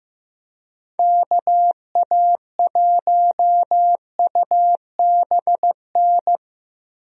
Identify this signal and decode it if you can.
Morse code, heard as a keyed tone, about 15 wpm, 700 Hz. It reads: KA1UBN
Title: USA Call Signs - 6 Characters - Repeated 2x - 15wpm